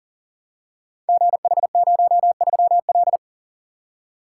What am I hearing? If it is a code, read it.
Morse code, heard as a keyed tone, about 42 wpm, 690 Hz. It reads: GH03L
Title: Code Groups - 5 Letter:Numbers - 40wpm